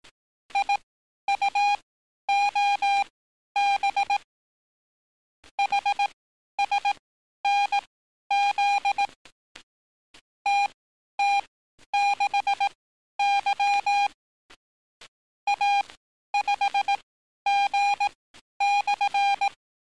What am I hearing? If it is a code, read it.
Morse code, heard as a keyed tone, about 18 wpm, spaced as for 11 wpm, 800 Hz. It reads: IUOB HSNZ TT6Y A5G/
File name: training / NAVPERS 11049to11060 / NAVPERS 11051 RA RB